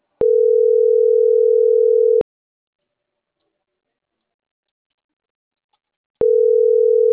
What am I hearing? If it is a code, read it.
Ringback tone (North America), 440 + 480 Hz.